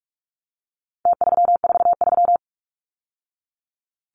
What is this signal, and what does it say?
Morse code, heard as a keyed tone, about 44 wpm, 700 Hz. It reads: T343
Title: Numbers - Contest Style - 4-Digit 45wpm